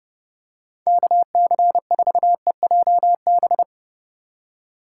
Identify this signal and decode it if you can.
Morse code, heard as a keyed tone, about 30 wpm, 700 Hz. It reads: KC4EJB
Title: USA Call Signs - 6x Character Spacing - 30wpm